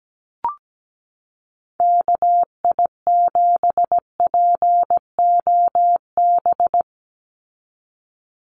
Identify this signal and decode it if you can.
Morse code, heard as a keyed tone, about 17 wpm, 700 Hz. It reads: KI7POB
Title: USA Call Signs 17wpm